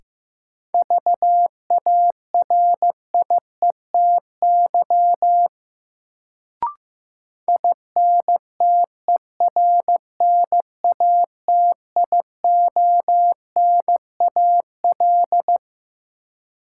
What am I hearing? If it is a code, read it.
Morse code, heard as a keyed tone, about 15 wpm, 700 Hz. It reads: VARIETY INTERNATIONAL
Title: Sentences from Top 1000 Words - Review Part 4 - New Word Drill - 15wpm